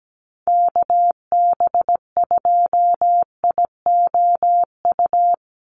Morse code, heard as a keyed tone, about 17 wpm, 700 Hz. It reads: KB2IOU
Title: USA Call Signs 17wpm